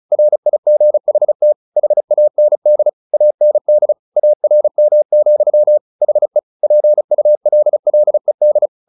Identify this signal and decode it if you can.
Morse code, heard as a keyed tone, about 34 wpm, 600 Hz. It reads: RIGHT HAND AND ARM, HE PULLED
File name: the old man and the sea 2 35wpm